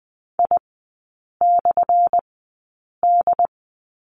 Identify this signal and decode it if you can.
Morse code, heard as a keyed tone, about 20 wpm, spaced as for 8 wpm, 700 Hz. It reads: I/D
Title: 178 Lesson 29e - Slash - Three Characters 20wpm